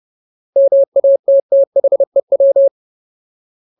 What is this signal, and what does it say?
Morse code, heard as a keyed tone, about 30 wpm, 550 Hz. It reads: MATTHEW
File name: Ultimate Word List / 30WPM Ultimate